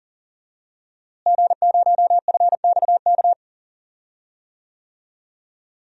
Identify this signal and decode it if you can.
Morse code, heard as a keyed tone, about 40 wpm, 700 Hz. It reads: G0FXK
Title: DXCC-100 Call Signs - 5 Characters - Repeated 2x - 40wpm